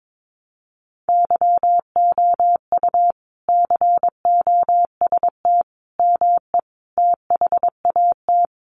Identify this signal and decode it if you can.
Morse code, heard as a keyed tone, about 22 wpm, 700 Hz. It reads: YOU COST ME THAT
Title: Sentences from Top 800 Words 22wpm